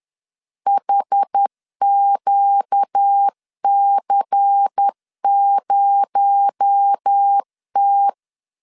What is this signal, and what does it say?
Morse code, heard as a keyed tone, about 10 wpm, 800 Hz. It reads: HQC0T
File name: CW / DET / 1 9 1500C S A 10wpm 4511